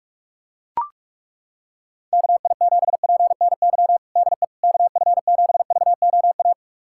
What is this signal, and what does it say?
Morse code, heard as a keyed tone, about 41 wpm, 690 Hz. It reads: KI7PNY DE KF7VOA
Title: USA Callsign DE Callsign - 6 Characters - 45wpm